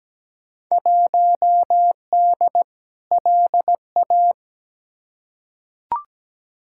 Morse code, heard as a keyed tone, about 17 wpm, 700 Hz. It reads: 1D LA
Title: Field Day Exchanges - 17wpm